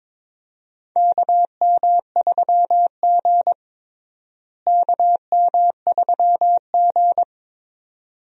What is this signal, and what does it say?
Morse code, heard as a keyed tone, about 22 wpm, 700 Hz. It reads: KM3G KM3G
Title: USA Call Signs - 4 Characters - Repeated 2x - 22wpm